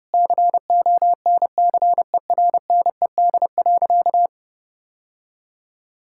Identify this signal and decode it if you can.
Morse code, heard as a keyed tone, about 30 wpm, 700 Hz. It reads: CONCERNED.
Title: Sentences from Top 1000 Words - Review Part 6 - 30wpm